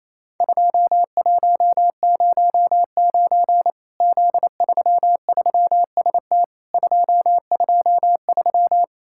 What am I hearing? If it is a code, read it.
Morse code, heard as a keyed tone, about 28 wpm, 700 Hz. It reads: 2109 Z33ST 223
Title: Worked All Europe DX Contest - QTC Exchange - 28wpm